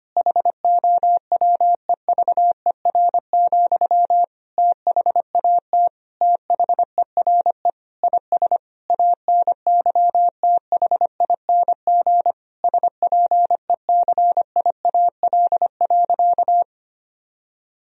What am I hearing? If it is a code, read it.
Morse code, heard as a keyed tone, about 25 wpm, 700 Hz. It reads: HOWEVER, THAT THERE IS ANYTHING SPECIAL.
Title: Sentences from Top 1000 Words - Review Part 3 - 25wpm